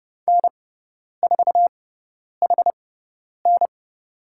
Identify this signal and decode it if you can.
Morse code, heard as a keyed tone, about 30 wpm, spaced as for 10 wpm, 700 Hz. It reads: N4HN
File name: Lesson 18g - Letter m - Call Signs 30wpm